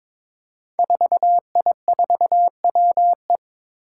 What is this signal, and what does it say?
Morse code, heard as a keyed tone, about 23 wpm, 700 Hz. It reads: 4I4WE
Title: Five Challenging Letter-Number Combinations 22wpm